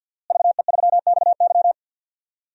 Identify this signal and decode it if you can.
Morse code, heard as a keyed tone, about 52 wpm, 700 Hz. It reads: VE3XY
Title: DXCC-100 Call Signs - 5 Characters - Repeated 3x - 50wpm